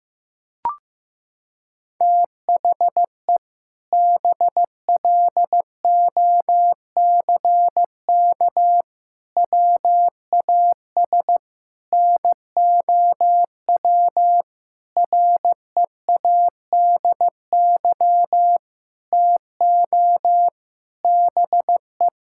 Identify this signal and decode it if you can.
Morse code, heard as a keyed tone, about 15 wpm, 700 Hz. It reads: THE BLOCK WAS NOW READY TO BE
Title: Sentences from Top 1600 Words 15wpm